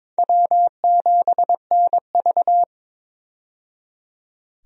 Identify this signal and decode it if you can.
Morse code, heard as a keyed tone, about 22 wpm, 700 Hz. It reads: W7NV
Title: USA Call Signs - 4 Characters - 22wpm